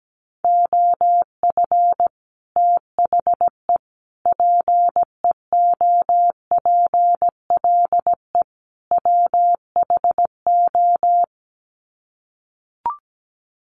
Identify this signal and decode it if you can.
Morse code, heard as a keyed tone, about 17 wpm, 700 Hz. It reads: OF THE PEOPLE WHO
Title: Sets of 4 Words 17wpm